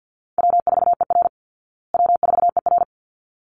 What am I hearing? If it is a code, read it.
Morse code, heard as a keyed tone, about 50 wpm, 700 Hz. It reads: W4ER W4ER
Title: Easy Call Signs Repeated 2x - 50wpm